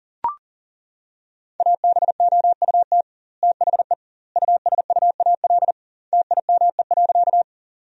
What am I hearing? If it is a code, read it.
Morse code, heard as a keyed tone, about 40 wpm, 700 Hz. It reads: ABOUT THE USUAL TIME.
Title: 2x Word-Spacing - Sentences from Top 400 Words - 40wpm